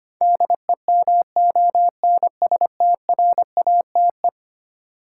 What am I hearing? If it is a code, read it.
Morse code, heard as a keyed tone, about 25 wpm, 700 Hz. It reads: DEMONSTRATE
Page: morse-code-ninja-practice-25wpm